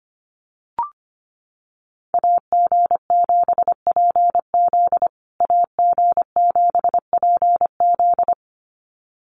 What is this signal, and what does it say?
Morse code, heard as a keyed tone, about 25 wpm, 700 Hz. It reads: AG7PZ AG7PZ
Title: USA Call Signs Repeated 2x 25wpm